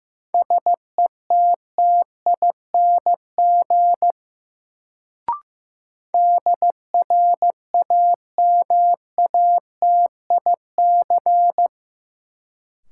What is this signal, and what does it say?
Morse code, heard as a keyed tone, about 15 wpm, 700 Hz. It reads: SETTING DRAMATIC
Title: Sentences from Top 1300 Words - New Word Drill Alt - 15wpm